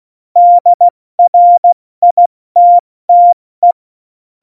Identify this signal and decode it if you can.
Morse code, heard as a keyed tone, about 16 wpm, spaced as for 14 wpm, 700 Hz. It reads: DRITTE
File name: Morsekurs / Witze / Witz 01. September 2020, 70 F80